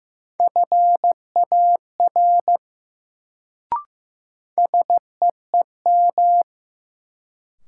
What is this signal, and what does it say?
Morse code, heard as a keyed tone, about 15 wpm, 700 Hz. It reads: FAR SEEM
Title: Sets of 3 Words - New Word Drill - 15wpm